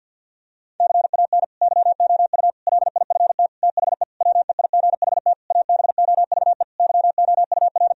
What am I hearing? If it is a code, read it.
Morse code, heard as a keyed tone, about 50 wpm, 700 Hz. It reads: CAN YOU LIFT THE WEIGHT ABOVE YOUR
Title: Sentences from Top 1000 Words - Review Part 3 - 50wpm